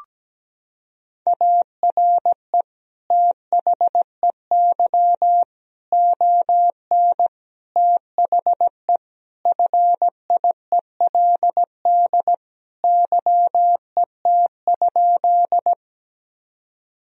Morse code, heard as a keyed tone, about 17 wpm, 700 Hz. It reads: ARE THEY ON THE FIELD YET?
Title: Sentences from Top 1000 Words - Review Part 2 - 17wpm